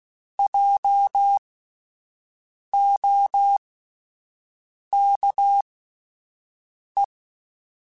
Morse code, heard as a keyed tone, about 16 wpm, spaced as for 5 wpm, 800 Hz. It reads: JOKE